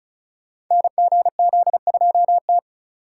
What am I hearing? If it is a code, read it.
Morse code, heard as a keyed tone, about 33 wpm, 700 Hz. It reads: NGZ2T
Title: Code Groups - 5 Letter:Numbers - 35wpm